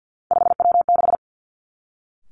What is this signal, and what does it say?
Morse code, heard as a keyed tone, about 50 wpm, 700 Hz. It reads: 5W6
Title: Three Challenging Letter-Number Combinations 50wpm